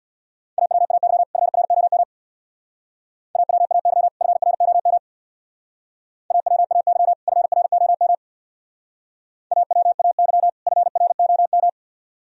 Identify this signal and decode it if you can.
Morse code, heard as a keyed tone, about 59 wpm, 700 Hz. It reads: AWAY FROM AWAY FROM AWAY FROM AWAY FROM
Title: Speed-Racing - Sets of 2 Words 50wpm